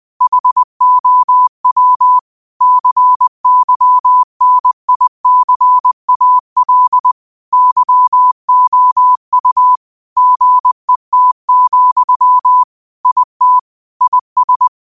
Morse code, heard as a keyed tone, about 20 wpm, 1000 Hz. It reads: HOW CYNICAL YOU GET, IT IS
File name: 20wpm